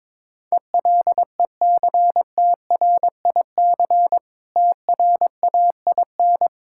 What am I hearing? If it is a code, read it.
Morse code, heard as a keyed tone, about 22 wpm, 690 Hz. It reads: ELECTRIC TRAIN
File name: Difficult Words with Hints - 22wpm